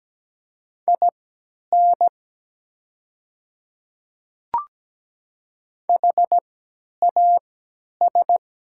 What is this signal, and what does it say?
Morse code, heard as a keyed tone, about 17 wpm, spaced as for 10 wpm, 700 Hz. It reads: IN HAS
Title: 3x Character Spacing - Top 100 Words - 17wpm